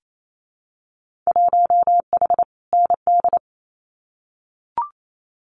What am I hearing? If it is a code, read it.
Morse code, heard as a keyed tone, about 28 wpm, 700 Hz. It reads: 1H ND
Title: Winter Field Day Exchanges 28wpm